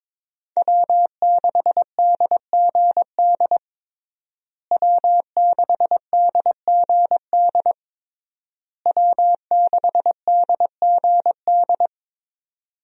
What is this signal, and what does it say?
Morse code, heard as a keyed tone, about 22 wpm, 700 Hz. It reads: W6DGD W6DGD W6DGD
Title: USA Call Signs - 5 Characters - Repeated 3x - 22wpm